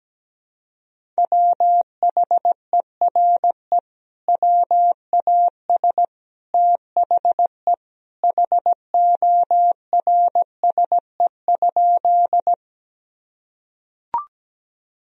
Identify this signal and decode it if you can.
Morse code, heard as a keyed tone, about 17 wpm, 700 Hz. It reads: WHERE WAS THE HORSE?
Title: Sentences from Top 1000 Words - Review Part 1 - 17wpm